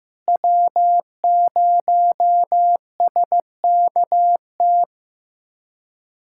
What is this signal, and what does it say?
Morse code, heard as a keyed tone, about 15 wpm, 700 Hz. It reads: W0SKT
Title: USA Callsign DE Callsign - 5 Characters - 15wpm